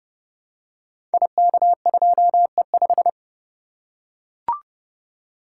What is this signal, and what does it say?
Morse code, heard as a keyed tone, about 30 wpm, 700 Hz. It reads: IK2E5
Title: Code Groups - 5 Letter:Numbers - 30wpm